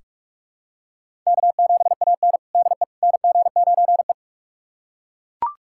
Morse code, heard as a keyed tone, about 41 wpm, 700 Hz. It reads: K7AN DE NG9E
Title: USA Callsign DE Callsign - 4 Characters - 45wpm